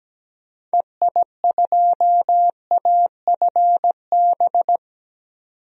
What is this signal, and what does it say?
Morse code, heard as a keyed tone, about 17 wpm, 700 Hz. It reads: EI2AFB
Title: DXCC-100 Call Signs - 6 Characters - 17wpm